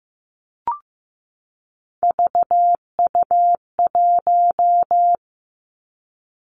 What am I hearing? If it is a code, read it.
Morse code, heard as a keyed tone, about 15 wpm, 700 Hz. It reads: VU1
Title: Three Challenging Letter-Number Combinations 15wpm